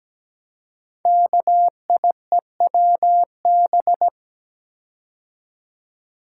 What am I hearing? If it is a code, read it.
Morse code, heard as a keyed tone, about 17 wpm, 700 Hz. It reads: KIEWB
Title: Code Groups - 5 Letters - 17wpm